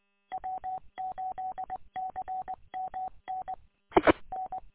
Morse code, heard as a keyed tone, about 20 wpm, 750 Hz. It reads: W8CMN IN
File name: w8cmn- -combined